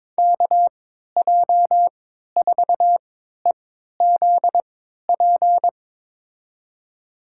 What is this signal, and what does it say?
Morse code, heard as a keyed tone, about 21 wpm, spaced as for 12 wpm, 700 Hz. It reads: KJ4EZP
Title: USA Call Signs - 6 Characters - 3x Character Spacing - 22wpm